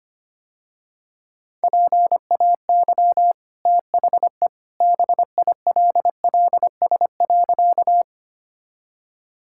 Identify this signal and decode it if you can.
Morse code, heard as a keyed tone, about 25 wpm, 700 Hz. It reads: PAY THE BILLS.
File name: Easy Sentences with Hints - 25wpm